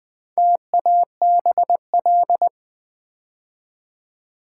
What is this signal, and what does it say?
Morse code, heard as a keyed tone, about 20 wpm, 700 Hz. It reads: TABL
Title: Build A Word - 2 Letters At A Time - 6-Letter Words - 20wpm